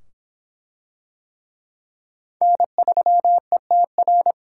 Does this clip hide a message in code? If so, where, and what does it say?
Yes: Morse code from 2.4 to 4.3 s, heard as a keyed tone, about 25 wpm, 700 Hz. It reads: N3ETR